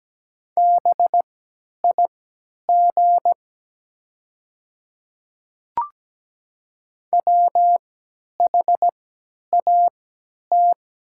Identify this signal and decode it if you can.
Morse code, heard as a keyed tone, about 17 wpm, spaced as for 10 wpm, 700 Hz. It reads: BIG WHAT
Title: 3x Character Spacing - Top 300 Words - 17wpm